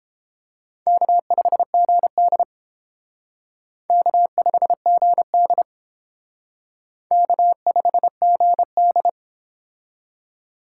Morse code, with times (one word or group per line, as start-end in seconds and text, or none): 0.9-2.4 K5GD
3.9-5.6 K5GD
7.1-9.1 K5GD